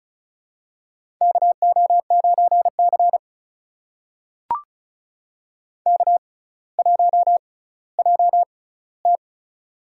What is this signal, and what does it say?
Morse code, heard as a keyed tone, about 36 wpm, spaced as for 18 wpm, 700 Hz. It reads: KO9C K1JT
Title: USA Call Signs - 4 Characters - 6x Character Spacing - 35wpm